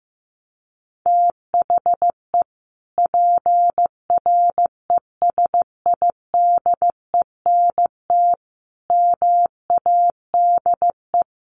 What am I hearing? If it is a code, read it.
Morse code, heard as a keyed tone, about 15 wpm, 700 Hz. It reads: THE PRESIDENT MADE
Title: Sentences from Top 1400 Words 15wpm